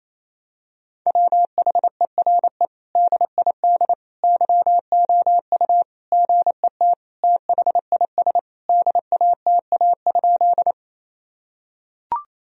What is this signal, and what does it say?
Morse code, heard as a keyed tone, about 27 wpm, 700 Hz. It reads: WHERE DID YOU GET THIS DATA?
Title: Sentences from Top 1000 Words - Review Part 3 - 28wpm